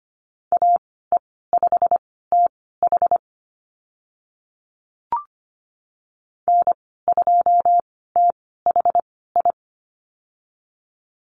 Morse code, heard as a keyed tone, about 25 wpm, spaced as for 16 wpm, 690 Hz. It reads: AE5TH N2THI
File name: Lesson 15g - Number 5 - Call Signs 25wpm